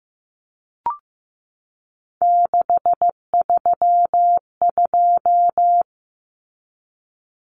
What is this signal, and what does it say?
Morse code, heard as a keyed tone, about 15 wpm, 700 Hz. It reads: 632
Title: Common Cut Numbers - 3-Digit 15wpm